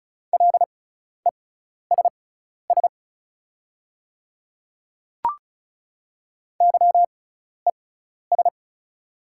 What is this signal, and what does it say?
Morse code, heard as a keyed tone, about 36 wpm, spaced as for 12 wpm, 690 Hz. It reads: LESS YES
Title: 6x Character Spacing - Top 500 Words - 35wpm